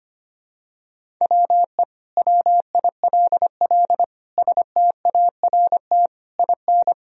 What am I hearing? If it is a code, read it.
Morse code, heard as a keyed tone, about 25 wpm, 700 Hz. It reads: WE WILL START IN
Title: Sentences from Top 1000 Words - Review Part 1 - 25wpm